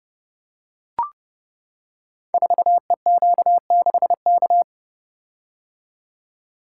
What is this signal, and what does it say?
Morse code, heard as a keyed tone, about 28 wpm, 700 Hz. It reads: 4EQ6K
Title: Code Groups - 5 Letter:Numbers - 30wpm